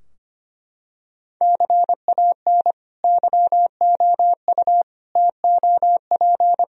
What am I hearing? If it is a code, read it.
Morse code, heard as a keyed tone, about 25 wpm, 700 Hz. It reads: CAN YOU TOP
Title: Sentences from Top 1000 Words - Review Part 2 - 25wpm